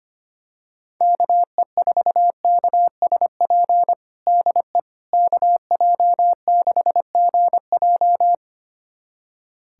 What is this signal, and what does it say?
Morse code, heard as a keyed tone, about 25 wpm, 700 Hz. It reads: KE4KSP DE KJ6GJ
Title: USA Callsign DE Callsign 25wpm